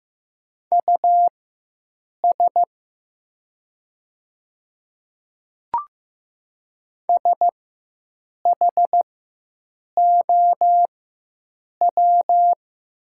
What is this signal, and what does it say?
Morse code, heard as a keyed tone, about 15 wpm, spaced as for 7 wpm, 700 Hz. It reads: US SHOW
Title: 4x Character Spacing - Top 200 Words - 15wpm